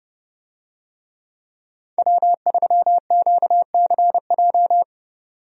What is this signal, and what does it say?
Morse code, heard as a keyed tone, about 30 wpm, 700 Hz. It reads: W3QCJ
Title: USA Call Signs - 5 Characters - 30wpm